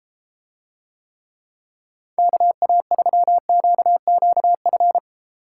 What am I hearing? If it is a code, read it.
Morse code, heard as a keyed tone, about 33 wpm, 700 Hz. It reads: KA3QQF